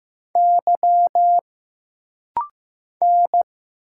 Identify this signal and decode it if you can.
Morse code, heard as a keyed tone, about 15 wpm, 700 Hz. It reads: Y N
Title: Single Letters - Rapid-Fire 15wpm